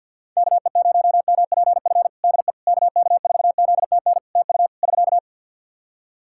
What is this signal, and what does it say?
Morse code, heard as a keyed tone, about 51 wpm, 700 Hz. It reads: KE0MPF DE KK4ZTN TU <SK>
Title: Practice QSO - Exchange 4 - Ending - 50wpm